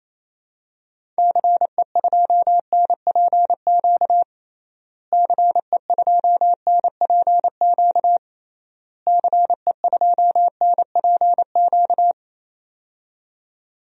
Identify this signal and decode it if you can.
Morse code, heard as a keyed tone, about 27 wpm, 700 Hz. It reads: CE2NPQ CE2NPQ CE2NPQ
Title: DXCC-100 Call Signs - 6 Characters - Repeated 3x - 28wpm